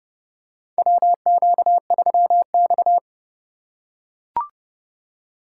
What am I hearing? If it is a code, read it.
Morse code, heard as a keyed tone, about 30 wpm, 700 Hz. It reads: WQ3X
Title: USA Call Signs - 4 Characters - 4x Character Spacing - 30wpm